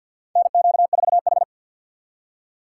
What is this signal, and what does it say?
Morse code, heard as a keyed tone, about 52 wpm, 700 Hz. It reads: NQ4H
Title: USA Call Signs - 3x Character Spacing - 50wpm